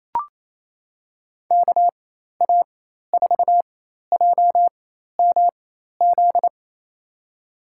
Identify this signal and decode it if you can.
Morse code, heard as a keyed tone, about 28 wpm, spaced as for 13 wpm, 700 Hz. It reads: KA4JMZ
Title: USA Call Signs - 6 Characters - 4x Character Spacing - 28wpm